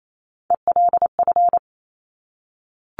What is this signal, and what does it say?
Morse code, heard as a keyed tone, about 28 wpm, 700 Hz. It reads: ELF
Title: Top Three Letter Combinations in Words 28wpm